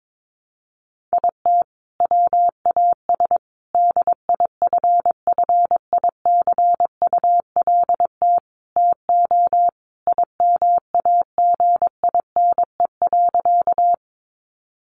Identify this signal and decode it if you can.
Morse code, heard as a keyed tone, about 22 wpm, 700 Hz. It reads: IT WAS DIFFICULT TO IMAGINE.
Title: Sentences from Top 1400 Words 22wpm